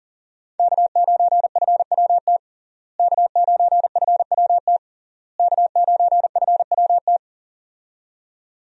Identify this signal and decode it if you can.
Morse code, heard as a keyed tone, about 37 wpm, 700 Hz. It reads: K9FWT K9FWT K9FWT
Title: USA Call Signs - 5 Characters - Repeated 3x - 40wpm